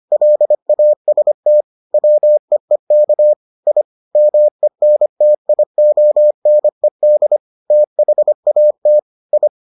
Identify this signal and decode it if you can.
Morse code, heard as a keyed tone, about 25 wpm, 600 Hz. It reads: LAST WEEK I MENTIONED THAT I